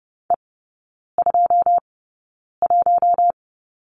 Morse code, heard as a keyed tone, about 30 wpm, spaced as for 9 wpm, 700 Hz. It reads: E21
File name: course / 30:10wpm / 087 Lesson 15e - Number 5 - Three Characters 30wpm